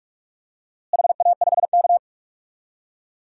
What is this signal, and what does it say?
Morse code, heard as a keyed tone, about 46 wpm, 700 Hz. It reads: HA5K
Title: DXCC-100 Call Signs - 4 Characters - Repeated 2x - 45wpm